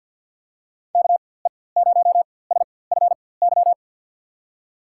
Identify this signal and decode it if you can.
Morse code, heard as a keyed tone, about 47 wpm, spaced as for 23 wpm, 700 Hz. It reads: KE0SFY
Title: USA Call Signs - 6 Characters - 4x Character Spacing - 50wpm